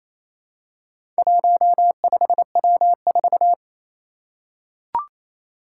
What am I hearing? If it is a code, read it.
Morse code, heard as a keyed tone, about 27 wpm, 700 Hz. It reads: 15W4
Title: Four Challenging Letter-Number Combinations 28wpm